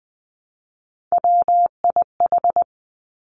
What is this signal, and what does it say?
Morse code, heard as a keyed tone, about 20 wpm, 700 Hz. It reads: WIH
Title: Three Challenging Letter-Number Combinations 20wpm